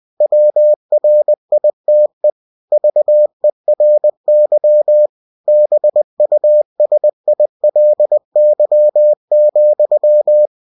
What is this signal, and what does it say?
Morse code, heard as a keyed tone, about 20 wpm, 600 Hz. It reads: WRITE VERY BUSILY,
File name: cw-books / winnie-the-pooh / winnie-20wpm-page-0062